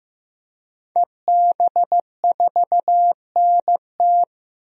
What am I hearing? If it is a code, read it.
Morse code, heard as a keyed tone, about 16 wpm, 700 Hz. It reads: EB4NT
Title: DXCC-100 Call Signs - 5 Characters - Repeated 2x - 15wpm